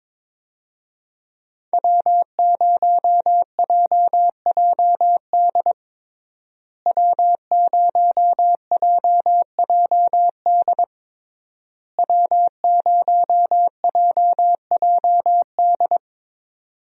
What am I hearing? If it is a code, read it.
Morse code, heard as a keyed tone, about 22 wpm, 700 Hz. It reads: W0JJD W0JJD W0JJD
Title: USA Call Signs - 5 Characters - Repeated 3x - 22wpm